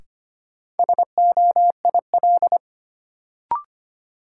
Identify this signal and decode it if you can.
Morse code, heard as a keyed tone, about 25 wpm, 700 Hz. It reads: SOIL
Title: Sentences from Top 1000 Words - Review Part 4 - New Word Drill - 25wpm